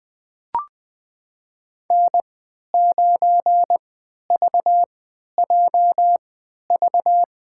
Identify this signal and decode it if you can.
Morse code, heard as a keyed tone, about 20 wpm, spaced as for 11 wpm, 700 Hz. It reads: N9VJV
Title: USA Call Signs - 5 Characters - 3x Character Spacing - 20wpm